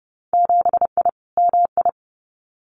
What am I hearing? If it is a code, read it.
Morse code, heard as a keyed tone, about 30 wpm, 700 Hz. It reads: 7I MI
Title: Winter Field Day Exchanges Encore 30wpm